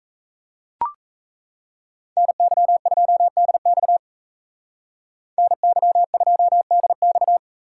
Morse code, heard as a keyed tone, about 37 wpm, 700 Hz. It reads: NY2DX NY2DX